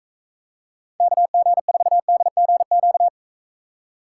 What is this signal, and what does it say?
Morse code, heard as a keyed tone, about 43 wpm, 700 Hz. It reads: KG4DGQ